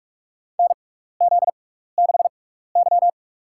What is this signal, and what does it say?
Morse code, heard as a keyed tone, about 47 wpm, spaced as for 16 wpm, 700 Hz. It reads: NZ6Y
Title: USA Call Signs - 6x Character Spacing - 45wpm